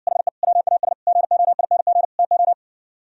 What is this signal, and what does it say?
Morse code, heard as a keyed tone, about 60 wpm, 700 Hz. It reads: HE WAS GOING TO